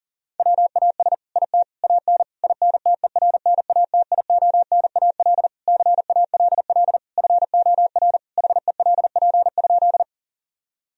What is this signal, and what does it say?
Morse code, heard as a keyed tone, about 40 wpm, 700 Hz. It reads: WAS IT AN INTERNATIONAL CALL FOR HELP?
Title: Sentences from Top 1000 Words - Review Part 4 - 40wpm